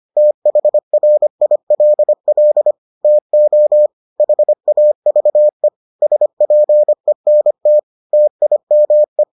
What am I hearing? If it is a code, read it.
Morse code, heard as a keyed tone, about 25 wpm, 600 Hz. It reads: THRILL TO HAVE SPENT TIME